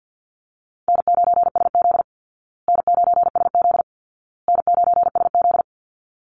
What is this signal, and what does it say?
Morse code, heard as a keyed tone, about 50 wpm, 700 Hz. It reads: N9SZ N9SZ N9SZ
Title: Easy Call Signs Repeated 3x - 50wpm